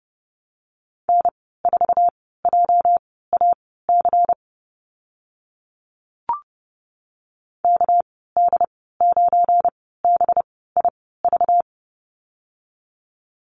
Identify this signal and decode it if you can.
Morse code, heard as a keyed tone, about 30 wpm, spaced as for 17 wpm, 700 Hz. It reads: N4JAC KD9BIV
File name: Lesson 37g - Letter x - Call Signs 30wpm